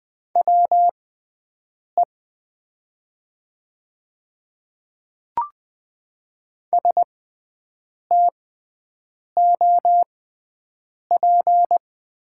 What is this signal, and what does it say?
Morse code, heard as a keyed tone, about 20 wpm, spaced as for 7 wpm, 700 Hz. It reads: WE STOP
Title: 6x Character Spacing - Top 300 Words - 20wpm